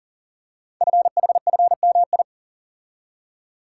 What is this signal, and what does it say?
Morse code, heard as a keyed tone, about 39 wpm, 690 Hz. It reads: FHFMI
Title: Code Groups - 5 Letter:Numbers - 40wpm